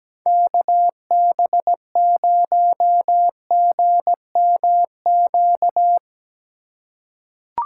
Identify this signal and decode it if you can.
Morse code, heard as a keyed tone, about 17 wpm, 700 Hz. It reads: KB0GMQ
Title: USA Call Signs - 4x Character Spacing - 17wpm